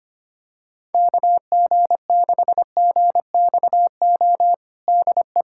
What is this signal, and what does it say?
Morse code, heard as a keyed tone, about 25 wpm, 700 Hz. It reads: KG6GXO DE
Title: USA Callsign DE Callsign - 6 Characters - 25wpm